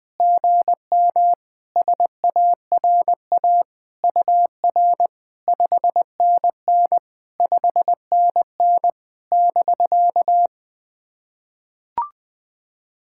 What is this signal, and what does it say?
Morse code, heard as a keyed tone, about 20 wpm, 700 Hz. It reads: GM SARA UR 5NN 5NN <BK>
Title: POTA QSO Practice - 20wpm